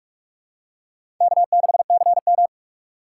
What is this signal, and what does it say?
Morse code, heard as a keyed tone, about 44 wpm, 700 Hz. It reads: K6CM
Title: USA Call Signs - 4 Characters - 4x Character Spacing - 45wpm